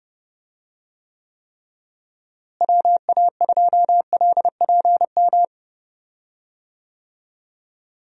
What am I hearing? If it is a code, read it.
Morse code, heard as a keyed tone, about 30 wpm, 700 Hz. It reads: WA2LPM
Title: USA Call Signs - 6 Characters - Repeated 3x - 30wpm